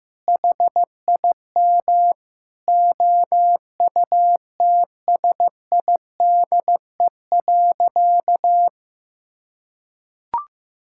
Morse code, heard as a keyed tone, about 15 wpm, 700 Hz. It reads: HIM OUTSIDE.
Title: Sentences from Top 1000 Words - Review Part 6 - 15wpm